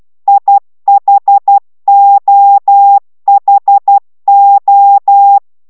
Morse code, heard as a keyed tone, about 12 wpm, 800 Hz. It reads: IHOHO